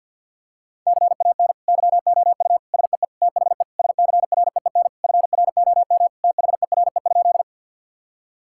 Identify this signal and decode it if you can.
Morse code, heard as a keyed tone, about 50 wpm, 700 Hz. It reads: CAN YOU SEE THE SCREEN FROM THERE?
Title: Sentences from Top 1000 Words - Review Part 6 - 50wpm